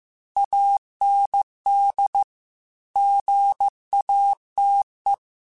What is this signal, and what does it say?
Morse code, heard as a keyed tone, about 15 wpm, 800 Hz. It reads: AND GATE
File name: Technical Subjects / Technical Subjects-mp3s / AND Gate @13